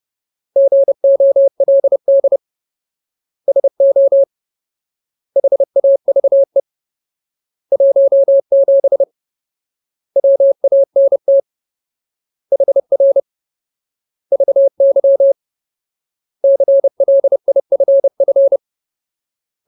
Morse code, heard as a keyed tone, about 30 wpm, 550 Hz. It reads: GOLD SO HAVE 17 WANT HR VY CLIFF
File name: Ultimate Word List / 30WPM Ultimate